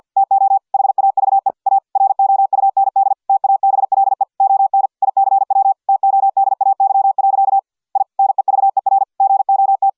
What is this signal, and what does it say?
Morse code, heard as a keyed tone, about 50 wpm, 780 Hz. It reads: TO HAVE A ROUND TABLE ON ICW TODAY. I NEVER GOT